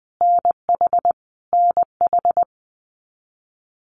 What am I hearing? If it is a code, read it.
Morse code, heard as a keyed tone, about 20 wpm, 700 Hz. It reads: NH NH
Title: US States Repeated 3x 20wpm